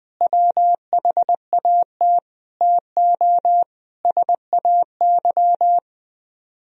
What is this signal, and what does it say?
Morse code, heard as a keyed tone, about 20 wpm, 700 Hz. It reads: WHAT TO SAY
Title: Speed-Racing - Sets of 5 Words 17wpm